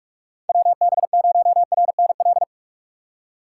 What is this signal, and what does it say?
Morse code, heard as a keyed tone, about 43 wpm, 700 Hz. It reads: WB0RNL
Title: USA Call Signs - 6 Characters - 4x Character Spacing - 45wpm